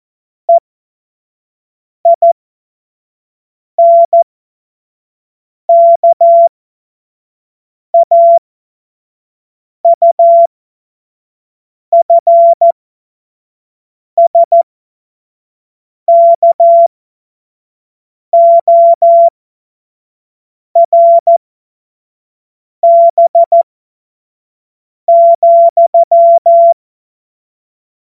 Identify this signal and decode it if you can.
Morse code, heard as a keyed tone, about 14 wpm, spaced as for 5 wpm, 680 Hz. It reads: EINKAUFSKORB,
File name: Morsekurs / Witze / Witz 17. August 2021, 25 F70